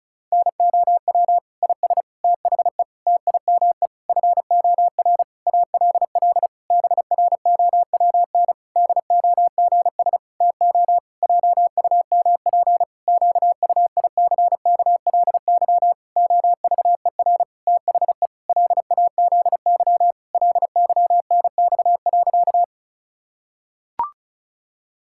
Morse code, heard as a keyed tone, about 36 wpm, 700 Hz. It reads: NOW IS THE TIME FOR ALL BROWN DOGS TO JUMP QUICKLY OVER THE LAZY LYNX.
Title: Pangram Challenge v2 - 35wpm